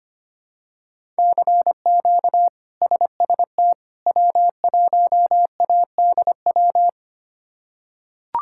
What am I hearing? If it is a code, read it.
Morse code, heard as a keyed tone, about 25 wpm, 700 Hz. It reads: CQ SST W1ADW
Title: K1USN SST QSO Practice - 25wpm